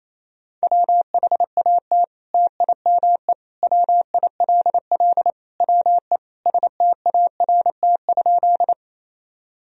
Morse code, heard as a keyed tone, about 28 wpm, 700 Hz. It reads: WHAT TIME WILL WE START?
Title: Sentences from Top 1000 Words - Review Part 1 - 28wpm